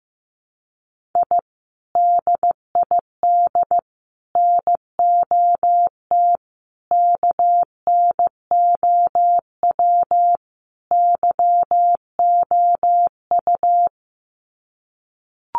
Morse code, heard as a keyed tone, about 15 wpm, 700 Hz. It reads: I DID NOT KNOW YOU
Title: Sets of 5 Words Encore2 15wpm